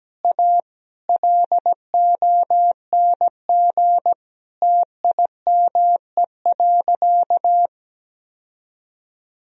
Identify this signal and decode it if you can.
Morse code, heard as a keyed tone, about 17 wpm, 700 Hz. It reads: A LONG TIME.
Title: Sentences from Top 1000 Words - Review Part 2 - 17wpm